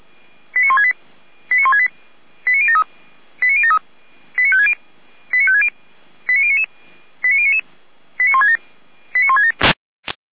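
Five-tone selective call (ZVEI), heard as quick runs of five tones, tones of about 70 ms. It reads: 89168 89168 89073 89073 89560 89560 89099 89099 89167 89167
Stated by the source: ZVEI five-tone selective call received on an SDR.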